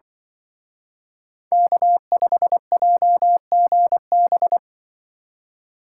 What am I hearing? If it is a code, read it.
Morse code, heard as a keyed tone, about 24 wpm, 700 Hz. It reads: K5JGB
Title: Speed Racing - USA 1x3 Call Signs 20wpm